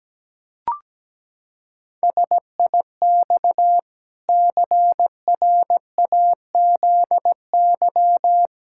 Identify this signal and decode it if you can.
Morse code, heard as a keyed tone, about 17 wpm, 700 Hz. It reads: SIX CRAZY
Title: Pangram Challenge v2 - 17wpm